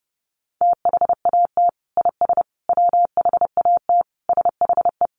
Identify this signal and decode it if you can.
Morse code, heard as a keyed tone, about 30 wpm, 700 Hz. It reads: THAT IS WHAT SHE
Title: Sentences from Top 100 Words 30wpm